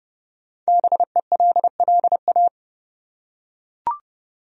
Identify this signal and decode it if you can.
Morse code, heard as a keyed tone, about 30 wpm, 700 Hz. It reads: BELLA
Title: USA Common Names - Max 5 Letters - 30wpm